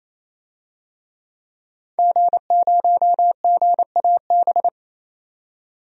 Morse code, heard as a keyed tone, about 28 wpm, 700 Hz. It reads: G0GAB
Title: DXCC-100 Call Signs - 5 Characters - 28wpm